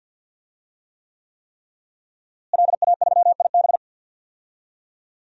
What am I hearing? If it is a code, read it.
Morse code, heard as a keyed tone, about 52 wpm, 700 Hz. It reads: LA3IB
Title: DXCC-100 Call Signs - 5 Characters - Repeated 3x - 50wpm